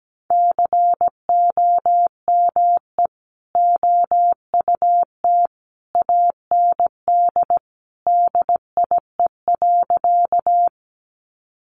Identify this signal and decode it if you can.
Morse code, heard as a keyed tone, about 17 wpm, 700 Hz. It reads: COME OUT AND DIE.
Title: Sentences from Top 1300 Words 17wpm